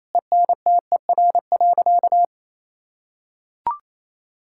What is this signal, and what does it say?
Morse code, heard as a keyed tone, about 28 wpm, 700 Hz. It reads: ENTER.
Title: Sentences from Top 1600 Words 28wpm